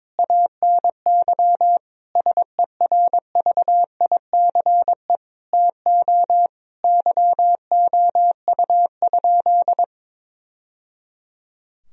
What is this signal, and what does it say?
Morse code, heard as a keyed tone, about 22 wpm, 700 Hz. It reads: ANY SERVICE TO YOU?
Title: Sentences from Top 1000 Words - Review Part 4 - 22wpm